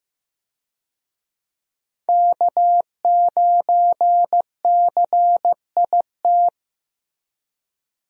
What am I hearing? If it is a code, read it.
Morse code, heard as a keyed tone, about 15 wpm, 700 Hz. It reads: K9CIT
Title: USA Call Signs - 5 Characters - 15wpm